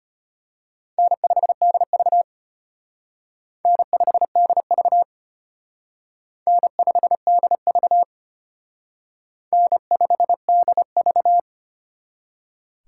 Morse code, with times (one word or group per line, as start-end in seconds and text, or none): 1.0-2.2 N5DV
3.6-5.0 N5DV
6.5-8.0 N5DV
9.5-11.4 N5DV